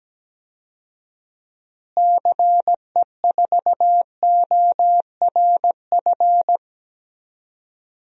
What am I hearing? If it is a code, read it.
Morse code, heard as a keyed tone, about 17 wpm, 700 Hz. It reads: CE4ORF
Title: DXCC-100 Call Signs - 6 Characters - Repeated 3x - 17wpm